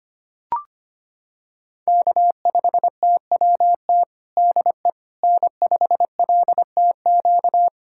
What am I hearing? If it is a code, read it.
Morse code, heard as a keyed tone, about 25 wpm, 700 Hz. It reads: K5TWT DE N5LTQ
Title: USA Callsign DE Callsign - 5 Characters - 25wpm